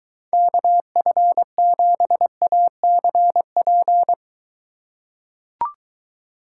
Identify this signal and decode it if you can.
Morse code, heard as a keyed tone, about 23 wpm, 700 Hz. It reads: KF7ACP